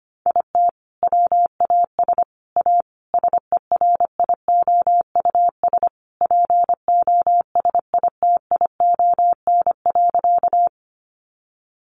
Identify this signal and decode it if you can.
Morse code, heard as a keyed tone, about 25 wpm, 700 Hz. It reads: IT WAS A SERIOUS POSITION.
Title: Sentences from Top 1100 Words 25wpm